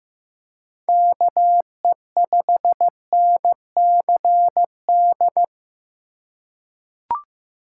Morse code, heard as a keyed tone, about 15 wpm, 700 Hz. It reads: KE5NCD
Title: USA Call Signs - 6 Characters - 2x Character Spacing - 15wpm